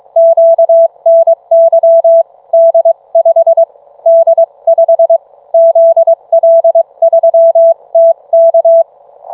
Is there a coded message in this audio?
Yes: Morse code from 0.2 to 8.8 s, heard as a keyed tone, about 22 wpm, 660 Hz. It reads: QNY D5 D5 ZL3TK